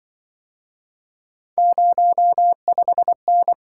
Morse code, heard as a keyed tone, about 24 wpm, 700 Hz. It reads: 05N
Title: Speed-Racing - Three Letter-Number Combinations - 20wpm